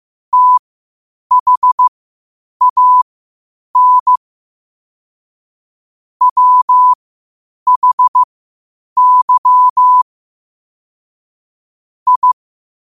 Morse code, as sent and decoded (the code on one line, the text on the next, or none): - .... .- -. / .-- .... -.-- / ..
THAN WHY I